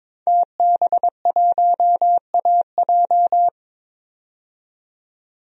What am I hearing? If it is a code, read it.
Morse code, heard as a keyed tone, about 22 wpm, 700 Hz. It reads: TB1AJ
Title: DXCC-100 Call Signs - 5 Characters - 22wpm